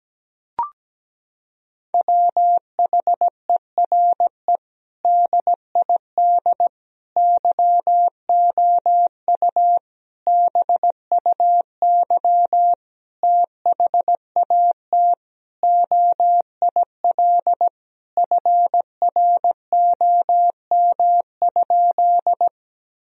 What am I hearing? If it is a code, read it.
Morse code, heard as a keyed tone, about 17 wpm, 700 Hz. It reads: WHERE DID YOU BUY THAT OIL FROM?
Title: Sentences from Top 1000 Words - Review Part 4 - 17wpm